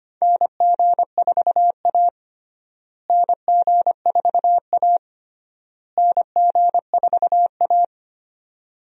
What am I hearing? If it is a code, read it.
Morse code, heard as a keyed tone, about 25 wpm, 700 Hz. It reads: NG4A NG4A NG4A
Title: USA Call Signs - 4 Characters - Repeated 3x - 25wpm